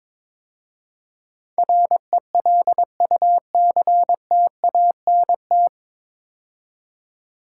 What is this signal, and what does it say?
Morse code, heard as a keyed tone, about 22 wpm, 700 Hz. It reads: RELUCTANT
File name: 9 Letter Words - First 3 Letters Given - 22wpm